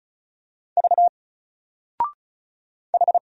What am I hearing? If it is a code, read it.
Morse code, heard as a keyed tone, about 34 wpm, 700 Hz. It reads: V H
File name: Single-Letter-Number-35wpm